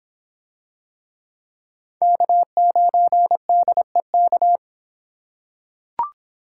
Morse code, heard as a keyed tone, about 24 wpm, 700 Hz. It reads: K9DEK